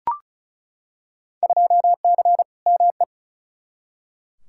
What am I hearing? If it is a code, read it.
Morse code, heard as a keyed tone, about 34 wpm, 700 Hz. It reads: 2C ME
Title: Field Day Exchanges Encore - 35wpm